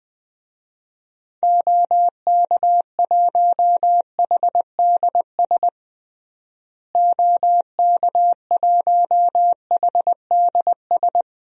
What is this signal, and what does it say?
Morse code, heard as a keyed tone, about 20 wpm, 700 Hz. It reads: OK1HDS OK1HDS
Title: DXCC-100 Call Signs - 6 Characters - Repeated 3x - 20wpm